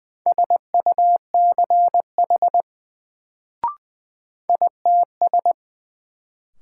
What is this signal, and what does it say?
Morse code, heard as a keyed tone, about 20 wpm, 700 Hz. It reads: SUCH ITS
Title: Sets of 2 Words - New Word Drill - 20wpm